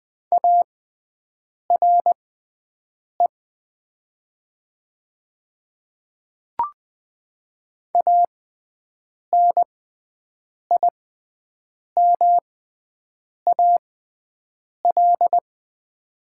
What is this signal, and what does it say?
Morse code, heard as a keyed tone, about 20 wpm, spaced as for 7 wpm, 700 Hz. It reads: ARE ANIMAL
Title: 6x Character Spacing - Top 200 Words - 20wpm